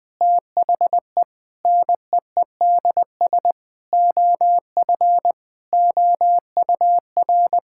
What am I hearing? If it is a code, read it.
Morse code, heard as a keyed tone, about 21 wpm, 700 Hz. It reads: THE NEEDS OF OUR
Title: Sentences with 5-Word Phrases - 20wpm